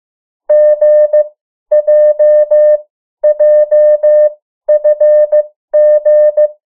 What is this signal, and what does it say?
Morse code, heard as a keyed tone, about 15 wpm, spaced as for 12 wpm, 600 Hz. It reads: GJJFG
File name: morse mp3 / 03 urok 2